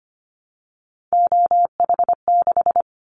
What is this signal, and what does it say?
Morse code, heard as a keyed tone, about 25 wpm, 700 Hz. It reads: OH6
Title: Callsign Prefixes for Top 50 DXCC - 25wpm